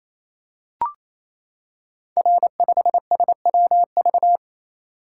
Morse code, heard as a keyed tone, about 27 wpm, 700 Hz. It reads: R5SWV
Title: Code Groups - 5 Letter:Numbers - 28wpm